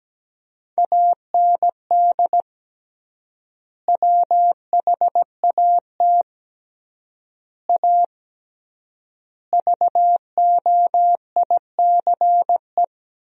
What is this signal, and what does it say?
Morse code, heard as a keyed tone, about 17 wpm, 700 Hz. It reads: AND WHAT A VOICE
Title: 3x Word-Spacing - Sentences from Top 500 Words - 17wpm